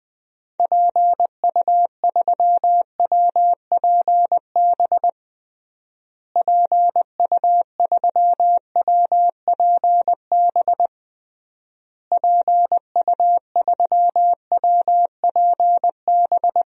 Morse code, heard as a keyed tone, about 20 wpm, 700 Hz. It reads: PU3WPB PU3WPB PU3WPB
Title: DXCC-100 Call Signs - 6 Characters - Repeated 3x - 20wpm